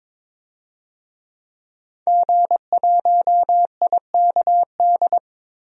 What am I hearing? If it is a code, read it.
Morse code, heard as a keyed tone, about 22 wpm, 700 Hz. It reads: G1IKD
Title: DXCC-100 Call Signs - 5 Characters - 22wpm